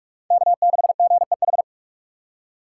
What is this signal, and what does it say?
Morse code, heard as a keyed tone, about 47 wpm, 690 Hz. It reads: K6GEH
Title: USA Call Signs - 5 Characters - 2x Character Spacing - 45wpm